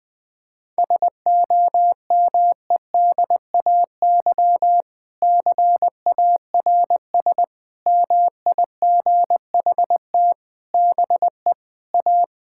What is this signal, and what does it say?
Morse code, heard as a keyed tone, about 20 wpm, 700 Hz. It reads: SOMEDAY CARS MIGHT BE A
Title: Sentences with 6-Word Phrases - 20wpm